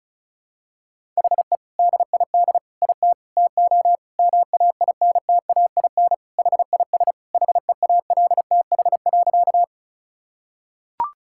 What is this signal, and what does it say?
Morse code, heard as a keyed tone, about 36 wpm, 700 Hz. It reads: HE DID IT TO MAINTAIN HIS HEALTH.
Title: Sentences from Top 1000 Words - Review Part 5 - 35wpm